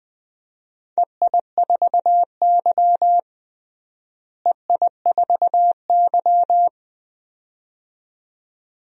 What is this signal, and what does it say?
Morse code, heard as a keyed tone, about 20 wpm, 700 Hz. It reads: EI4Y EI4Y
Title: DXCC-100 Call Signs - 4 Characters - Repeated 2x - 20wpm